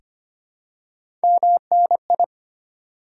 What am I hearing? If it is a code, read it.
Morse code, heard as a keyed tone, about 25 wpm, 700 Hz. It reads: MNI
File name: Top 90 QSO Elements - 2x Character Spacing - 25wpm